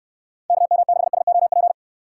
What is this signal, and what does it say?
Morse code, heard as a keyed tone, about 56 wpm, 700 Hz. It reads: KM6SOP
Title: Speed Racing - USA 2x3 Call Signs 45wpm